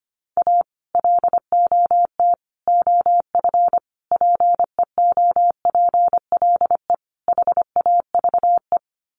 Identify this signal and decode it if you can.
Morse code, heard as a keyed tone, about 25 wpm, 700 Hz. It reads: A LOT OF PEOPLE HAVE
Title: Sets of 5 Words 25wpm